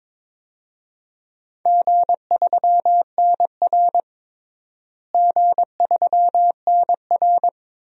Morse code, heard as a keyed tone, about 21 wpm, 700 Hz. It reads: G3NR G3NR
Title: DXCC-100 Call Signs - 4 Characters - Repeated 2x - 22wpm